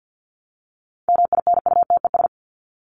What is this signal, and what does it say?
Morse code, heard as a keyed tone, about 50 wpm, 700 Hz. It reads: MINUTES
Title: ES:S Suffix - Top 100 Words with Base Words 50wpm